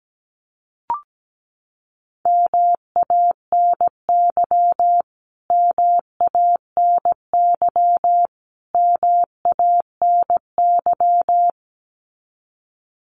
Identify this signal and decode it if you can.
Morse code, heard as a keyed tone, about 17 wpm, 700 Hz. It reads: MANY MANY MANY
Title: Top 500 Words Repeated 17wpm